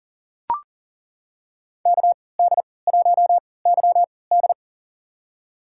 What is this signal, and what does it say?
Morse code, heard as a keyed tone, about 42 wpm, spaced as for 23 wpm, 700 Hz. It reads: KD1YD
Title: USA Call Signs - 5 Characters - 3x Character Spacing - 40wpm